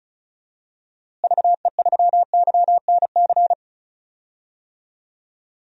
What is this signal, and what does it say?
Morse code, heard as a keyed tone, about 34 wpm, 700 Hz. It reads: VE3YNC
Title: DXCC-100 Call Signs - 6 Characters - Repeated 2x - 35wpm